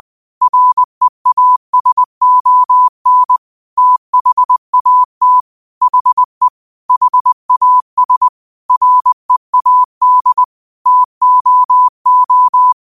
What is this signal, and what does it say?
Morse code, heard as a keyed tone, about 20 wpm, 1000 Hz. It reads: REASON THAT HE HAS READ TOO